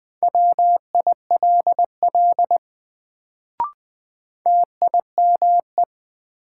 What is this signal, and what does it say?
Morse code, heard as a keyed tone, about 20 wpm, 700 Hz. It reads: WILL TIME
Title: Sets of 2 Words - New Word Drill - 20wpm